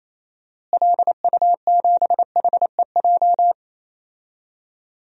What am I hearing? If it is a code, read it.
Morse code, heard as a keyed tone, about 28 wpm, 700 Hz. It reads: LU7HEJ
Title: DXCC-100 Call Signs - 6 Characters - Repeated 3x - 28wpm